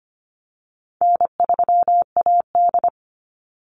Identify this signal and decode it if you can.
Morse code, heard as a keyed tone, about 25 wpm, 700 Hz. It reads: N3AD
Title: Call Signs - 25wpm